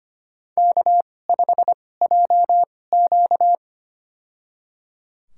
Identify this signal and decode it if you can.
Morse code, heard as a keyed tone, about 25 wpm, spaced as for 18 wpm, 700 Hz. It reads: K5JQ
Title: USA Call Signs - 4 Characters - 2x Character Spacing - 25wpm